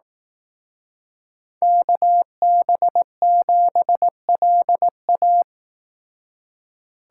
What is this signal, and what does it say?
Morse code, heard as a keyed tone, about 18 wpm, 700 Hz. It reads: KB7LA
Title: Speed Racing - USA 2x2 Call Signs 15wpm